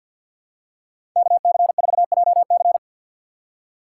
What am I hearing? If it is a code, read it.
Morse code, heard as a keyed tone, about 52 wpm, 700 Hz. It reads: KC4JC
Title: USA Call Signs - 5 Characters - 4x Character Spacing - 50wpm